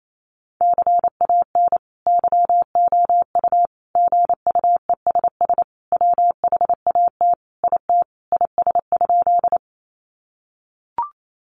Morse code, heard as a keyed tone, about 28 wpm, 700 Hz. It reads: CAN YOU GUESS WHAT IT IS?
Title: Sentences from Top 1500 Words 28wpm